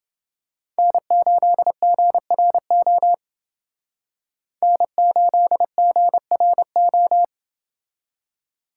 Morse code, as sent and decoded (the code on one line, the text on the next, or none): -. ---.. --. .-. --- / -. ---.. --. .-. ---
N8GRO N8GRO